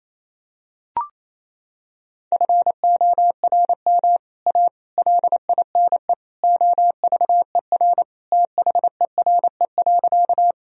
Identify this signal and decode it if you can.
Morse code, heard as a keyed tone, about 28 wpm, 700 Hz. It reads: FORM A LINE OVER THERE.
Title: Sentences from Top 1000 Words - Review Part 1 - 28wpm